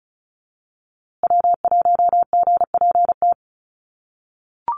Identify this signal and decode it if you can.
Morse code, heard as a keyed tone, about 34 wpm, 700 Hz. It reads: W1GPT
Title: USA Call Signs Repeated 3x 35wpm